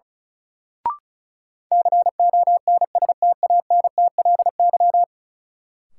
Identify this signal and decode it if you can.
Morse code, heard as a keyed tone, about 36 wpm, 700 Hz. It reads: CONSTANTLY
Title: Sentences from Top 1600 Words - New Word Drill Alt - 35wpm